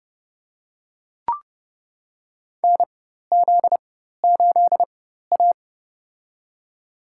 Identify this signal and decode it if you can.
Morse code, heard as a keyed tone, about 31 wpm, spaced as for 14 wpm, 700 Hz. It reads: NZ8A
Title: USA Call Signs - 4 Characters - 4x Character Spacing - 30wpm